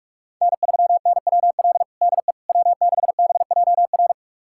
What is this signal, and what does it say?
Morse code, heard as a keyed tone, about 46 wpm, 700 Hz. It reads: N3NWL DE W6BJR
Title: USA Callsign DE Callsign - 5 Characters - 45wpm